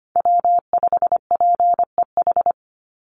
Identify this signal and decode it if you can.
Morse code, heard as a keyed tone, about 25 wpm, 700 Hz. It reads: W5PEH
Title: Call Signs Repeated 3x - 25wpm